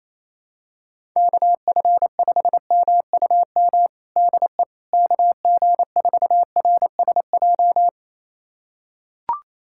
Morse code, heard as a keyed tone, about 28 wpm, 700 Hz. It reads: KF5MUM DE KG4RSJ
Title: USA Callsign DE Callsign - 6 Characters - 28wpm